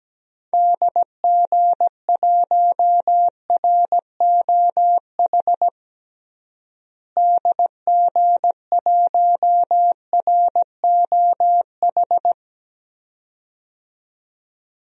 Morse code, heard as a keyed tone, about 17 wpm, 700 Hz. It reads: DG1ROH DG1ROH
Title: DXCC-100 Call Signs - 6 Characters - Repeated 2x - 17wpm